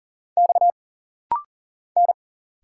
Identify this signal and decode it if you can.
Morse code, heard as a keyed tone, about 40 wpm, 700 Hz. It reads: X N
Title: Single Letters - Rapid-Fire 40wpm